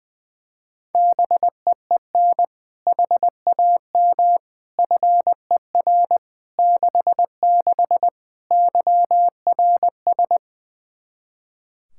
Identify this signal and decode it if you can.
Morse code, heard as a keyed tone, about 21 wpm, 700 Hz. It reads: BEEN HAM FER 66 YRS
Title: Practice QSO - Exchange 3 - More Info - 20wpm